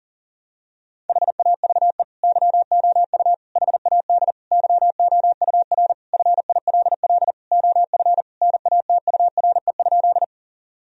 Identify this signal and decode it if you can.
Morse code, heard as a keyed tone, about 40 wpm, 700 Hz. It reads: HAVE YOU HAD YOUR FILL OF NATURE?
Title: Sentences from Top 1000 Words - Review Part 3 - 40wpm